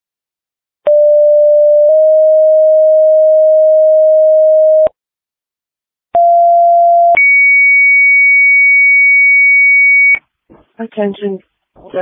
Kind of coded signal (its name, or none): two-tone page